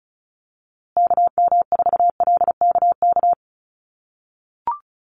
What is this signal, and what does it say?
Morse code, heard as a keyed tone, about 34 wpm, 700 Hz. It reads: KM4LKK
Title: USA Call Signs Repeated 2x 35wpm